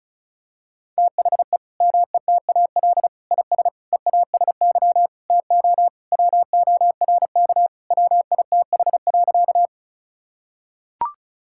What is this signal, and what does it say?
Morse code, heard as a keyed tone, about 36 wpm, 700 Hz. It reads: THE METAL IS EASY TO WORK WITH.
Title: Sentences from Top 1000 Words - Review Part 6 - 35wpm